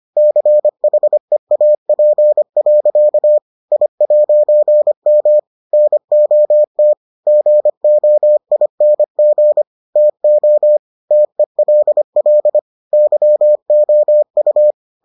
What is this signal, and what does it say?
Morse code, heard as a keyed tone, about 25 wpm, 600 Hz. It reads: CHEAP. I'M NOT GOING TO TELL YOU